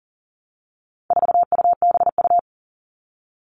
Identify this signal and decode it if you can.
Morse code, heard as a keyed tone, about 39 wpm, 700 Hz. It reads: 4UBU
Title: All Cut Numbers - 4-Digit 40wpm